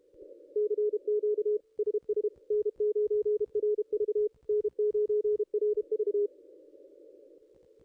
Morse code, heard as a keyed tone, about 30 wpm, 440 Hz, with radio noise. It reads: CQ SS N9RV N9RV
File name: contests / ss-cw-k5tr-op-n5rz -2200